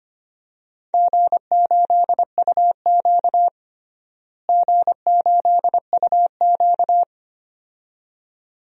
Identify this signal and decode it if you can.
Morse code, heard as a keyed tone, about 25 wpm, 700 Hz. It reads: G8UQ G8UQ
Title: DXCC-100 Call Signs - 4 Characters - Repeated 2x - 25wpm